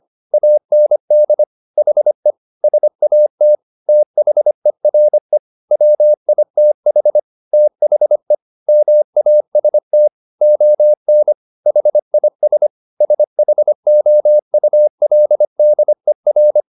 Morse code, heard as a keyed tone, about 25 wpm, 600 Hz. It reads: AND HE SAT THERE WITH THE MAST ON HIS SHOULDER